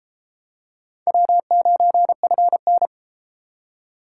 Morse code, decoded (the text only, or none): W9FN